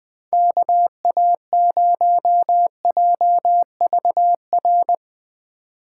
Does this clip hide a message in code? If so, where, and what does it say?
Yes: Morse code from 0.3 to 4.9 s, heard as a keyed tone, about 20 wpm, 700 Hz. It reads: KA0JVR